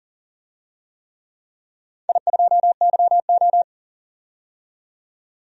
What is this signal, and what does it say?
Morse code, heard as a keyed tone, about 40 wpm, 700 Hz. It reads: I2YO
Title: DXCC-100 Call Signs - 4 Characters - 40wpm